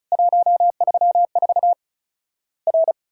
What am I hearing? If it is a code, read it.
Morse code, heard as a keyed tone, about 36 wpm, 700 Hz. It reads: 134 R
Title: Worked All Europe DX Contest - QTC Exchange - 35wpm